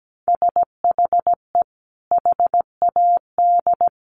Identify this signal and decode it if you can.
Morse code, heard as a keyed tone, about 17 wpm, 700 Hz. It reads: SHE HAD
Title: Sentences from Top 1200 Words 17wpm